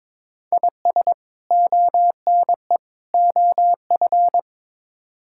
Morse code, heard as a keyed tone, about 22 wpm, 700 Hz. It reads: IS ONE OF
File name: practice / 22wpm / Build A Phrase - 5-Word Phrases - 22wpm